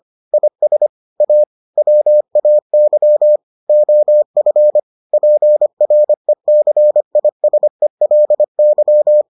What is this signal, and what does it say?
Morse code, heard as a keyed tone, about 25 wpm, 600 Hz. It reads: IS A WAY OF PRECISELY